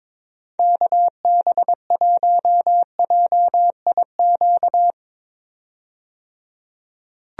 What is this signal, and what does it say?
Morse code, heard as a keyed tone, about 21 wpm, 700 Hz. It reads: KB1JIQ
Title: USA Call Signs - 6 Characters - 22wpm